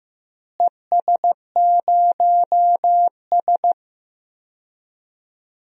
Morse code, heard as a keyed tone, about 15 wpm, 700 Hz. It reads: ES0S
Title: DXCC-100 Call Signs - 4 Characters - Repeated 3x - 15wpm